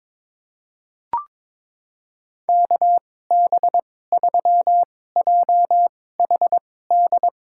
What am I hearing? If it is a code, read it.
Morse code, heard as a keyed tone, about 22 wpm, spaced as for 16 wpm, 700 Hz. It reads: KB3JHD
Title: USA Call Signs - 6 Characters - 2x Character Spacing - 22wpm